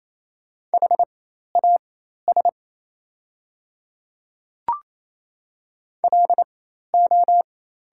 Morse code, heard as a keyed tone, about 28 wpm, spaced as for 13 wpm, 700 Hz. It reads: HAS LO
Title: 4x Character Spacing - Top 100 Words - 28wpm